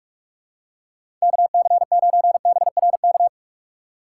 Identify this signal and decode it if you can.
Morse code, heard as a keyed tone, about 46 wpm, 700 Hz. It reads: KC9BRK
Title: USA Call Signs - 6 Characters - 4x Character Spacing - 45wpm